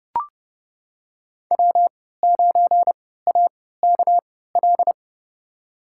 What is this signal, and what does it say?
Morse code, heard as a keyed tone, about 30 wpm, spaced as for 17 wpm, 700 Hz. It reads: W9AKL
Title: USA Call Signs - 5 Characters - 3x Character Spacing - 30wpm